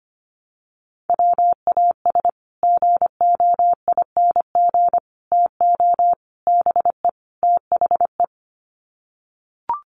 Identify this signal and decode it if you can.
Morse code, heard as a keyed tone, about 25 wpm, 700 Hz. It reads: WAS GOING TO BE THE
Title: Sets of 5 Words Encore2 25wpm